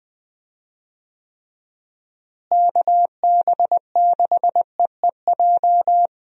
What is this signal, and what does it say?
Morse code, heard as a keyed tone, about 20 wpm, 700 Hz. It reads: KB6EEJ
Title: USA Call Signs - 6 Characters - Repeated 2x - 20wpm